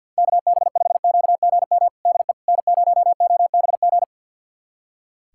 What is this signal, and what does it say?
Morse code, heard as a keyed tone, about 50 wpm, 700 Hz. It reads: KB5QGM DE N0OBG
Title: USA Callsign DE Callsign 50wpm